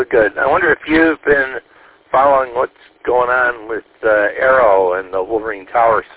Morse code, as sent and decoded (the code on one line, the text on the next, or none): none